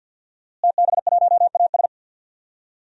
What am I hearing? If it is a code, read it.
Morse code, heard as a keyed tone, about 52 wpm, 700 Hz. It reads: TB1AS
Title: DXCC-100 Call Signs - 5 Characters - Repeated 2x - 50wpm